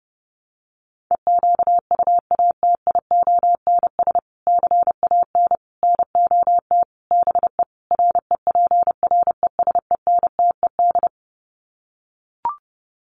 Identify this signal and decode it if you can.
Morse code, heard as a keyed tone, about 28 wpm, 700 Hz. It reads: EQUATIONS CAN NOT BE REPRESENTED
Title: Sets of 5 Words 30wpm